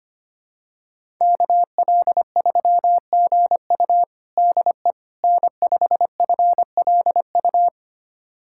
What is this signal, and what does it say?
Morse code, heard as a keyed tone, about 25 wpm, 700 Hz. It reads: KL3GU DE N5FLU
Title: USA Callsign DE Callsign - 5 Characters - 25wpm